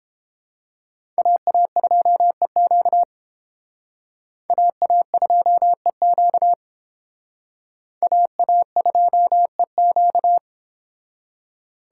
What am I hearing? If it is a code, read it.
Morse code, heard as a keyed tone, about 30 wpm, 700 Hz. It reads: AA2EQ AA2EQ AA2EQ